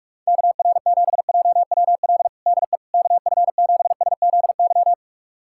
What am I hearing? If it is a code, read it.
Morse code, heard as a keyed tone, about 41 wpm, 700 Hz. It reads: KR7JWL DE KF7SZY
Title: USA Callsign DE Callsign - 6 Characters - 45wpm